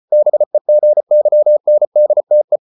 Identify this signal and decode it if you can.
Morse code, heard as a keyed tone, about 34 wpm, 600 Hz. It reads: BEGYNDTE
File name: Ferie QRQ 34wpm